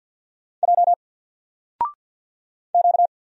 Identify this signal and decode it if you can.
Morse code, heard as a keyed tone, about 53 wpm, 700 Hz. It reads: J Q
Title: Single Letter-Number Alt 50wpm